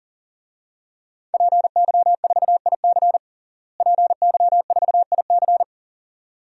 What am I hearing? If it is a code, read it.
Morse code, heard as a keyed tone, about 37 wpm, 700 Hz. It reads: PY4IC PY4IC
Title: DXCC-100 Call Signs - 5 Characters - Repeated 2x - 40wpm